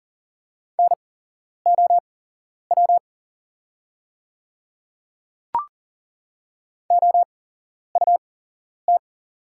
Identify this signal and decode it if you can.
Morse code, heard as a keyed tone, about 40 wpm, spaced as for 11 wpm, 690 Hz. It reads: NOW OUT
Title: 8x Character Spacing - Top 100 Words - 40wpm